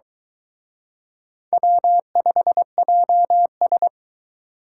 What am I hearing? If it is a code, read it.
Morse code, heard as a keyed tone, about 23 wpm, 700 Hz. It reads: W5JS